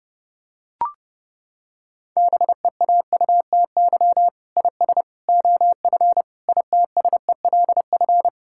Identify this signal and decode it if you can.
Morse code, heard as a keyed tone, about 32 wpm, 700 Hz. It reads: BEAUTY IS OF ITSELF
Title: Sentences from Top 1000 Words - Review Part 5 - 30wpm